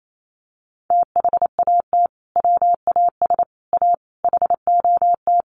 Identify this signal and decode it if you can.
Morse code, heard as a keyed tone, about 28 wpm, 700 Hz. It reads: THAT WAS A HOT
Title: Sentences from Top 1100 Words 28wpm